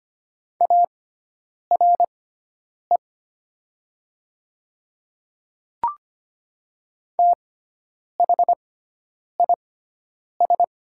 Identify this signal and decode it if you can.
Morse code, heard as a keyed tone, about 25 wpm, spaced as for 9 wpm, 700 Hz. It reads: ARE THIS
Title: 6x Character Spacing - Top 100 Words - 25wpm